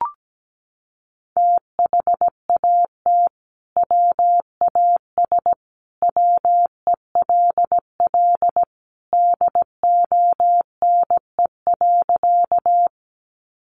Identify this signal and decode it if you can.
Morse code, heard as a keyed tone, about 17 wpm, 700 Hz. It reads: THAT WAS WELL DONE.
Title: Sentences from Top 500 Words 17wpm